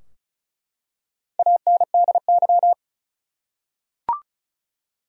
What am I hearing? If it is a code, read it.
Morse code, heard as a keyed tone, about 35 wpm, 700 Hz. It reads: ANDY
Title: USA Common Names - Max 4 Letters - 35wpm